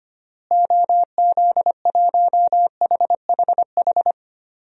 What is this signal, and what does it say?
Morse code, heard as a keyed tone, about 25 wpm, 700 Hz. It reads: OZ1HHH
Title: DXCC-100 Call Signs - 6 Characters - Repeated 2x - 25wpm